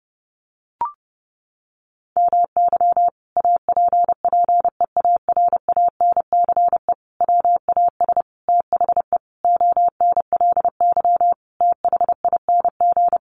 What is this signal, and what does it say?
Morse code, heard as a keyed tone, about 30 wpm, 700 Hz. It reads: MY APPEARANCE WAS THE ONLY THING
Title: Sentences from Top 1100 Words 30wpm